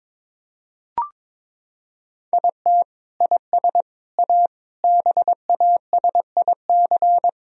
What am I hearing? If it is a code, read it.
Morse code, heard as a keyed tone, about 22 wpm, 700 Hz. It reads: IT IS A BASIC
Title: Sentences from Top 1000 Words - Review Part 3 - 22wpm